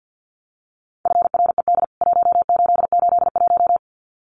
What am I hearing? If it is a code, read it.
Morse code, heard as a keyed tone, about 50 wpm, 700 Hz. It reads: FRED 1881
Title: CWT Exchange 50wpm